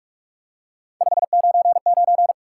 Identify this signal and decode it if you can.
Morse code, heard as a keyed tone, about 46 wpm, 700 Hz. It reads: 599
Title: Top 90 QSO Elements 45wpm